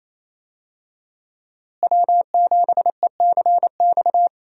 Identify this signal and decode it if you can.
Morse code, heard as a keyed tone, about 28 wpm, 700 Hz. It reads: W7ECX
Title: USA Call Signs - 5 Characters - Repeated 2x - 28wpm